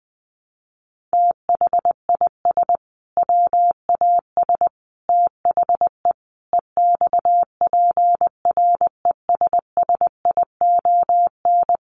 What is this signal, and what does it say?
Morse code, heard as a keyed tone, about 20 wpm, 700 Hz. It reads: THIS WAS THE EXPRESSION
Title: Sentences from Top 1200 Words 20wpm